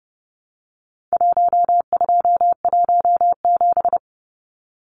Morse code, heard as a keyed tone, about 30 wpm, 700 Hz. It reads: 1217
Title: Numbers - Contest Style - 4-Digit 30wpm